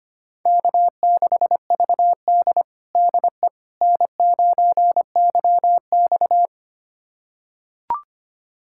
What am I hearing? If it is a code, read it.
Morse code, heard as a keyed tone, about 25 wpm, 700 Hz. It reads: K6VD DE N9YX
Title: USA Callsign DE Callsign - 4 Characters - 25wpm